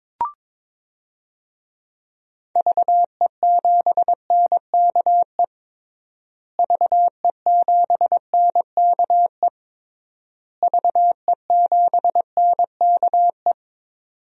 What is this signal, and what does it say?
Morse code, heard as a keyed tone, about 22 wpm, 700 Hz. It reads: VE7NKE VE7NKE VE7NKE
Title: DXCC-100 Call Signs - 6 Characters - Repeated 3x - 22wpm